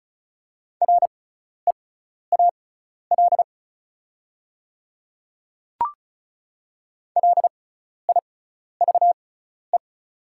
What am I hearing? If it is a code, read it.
Morse code, heard as a keyed tone, about 36 wpm, spaced as for 12 wpm, 700 Hz. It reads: REAL LIVE
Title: 6x Character Spacing - Top 300 Words - 35wpm